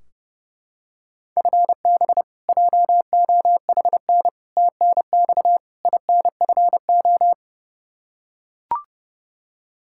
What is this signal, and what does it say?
Morse code, heard as a keyed tone, about 30 wpm, 700 Hz. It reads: FB JOHN TNX INFO
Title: Practice QSO - Exchange 3 - More Info - 30wpm